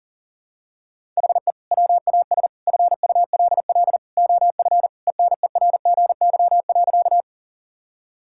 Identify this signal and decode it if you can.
Morse code, heard as a keyed tone, about 40 wpm, 700 Hz. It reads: HE WAS FULL OF ENERGY.
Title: Sentences from Top 1000 Words - Review Part 6 - 40wpm